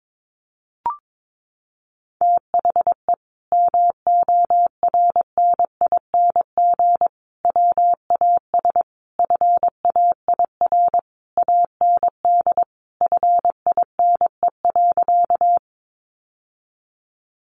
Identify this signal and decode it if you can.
Morse code, heard as a keyed tone, about 22 wpm, 700 Hz. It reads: THE MORNING WAS FAIR AND FINE.
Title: Sentences from Top 1300 Words 22wpm